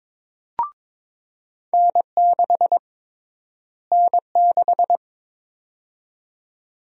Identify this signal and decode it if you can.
Morse code, heard as a keyed tone, about 22 wpm, 700 Hz. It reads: N6 N6
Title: USA Call Sign Prefixes - Repeated 2x - 22wpm